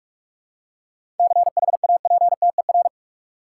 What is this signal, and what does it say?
Morse code, heard as a keyed tone, about 41 wpm, 700 Hz. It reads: CHAPTER
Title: Seven Letter Words - 45wpm